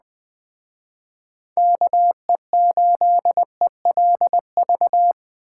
Morse code, heard as a keyed tone, about 20 wpm, 700 Hz. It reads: KE8ELV
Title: Speed Racing - USA 2x3 Call Signs 17wpm